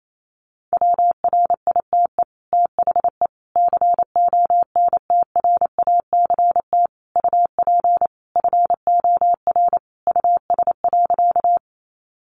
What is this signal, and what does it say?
Morse code, heard as a keyed tone, about 28 wpm, 700 Hz. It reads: WRITE THE CONTRACT UP FOR US.
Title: Sentences from Top 1400 Words 28wpm